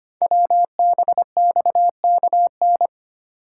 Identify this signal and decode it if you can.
Morse code, heard as a keyed tone, about 25 wpm, 700 Hz. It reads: WBXKN
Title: Code Groups - 5 Letters - 25wpm